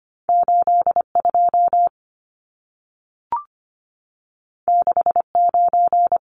Morse code, heard as a keyed tone, about 25 wpm, 700 Hz. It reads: 82 69
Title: Numbers - 2-Digit 25wpm